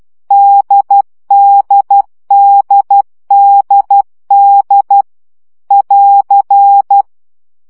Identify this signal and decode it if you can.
Morse code, heard as a keyed tone, about 12 wpm, 800 Hz. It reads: DDDDD <AR>
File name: CW gyakorlas d 10 baud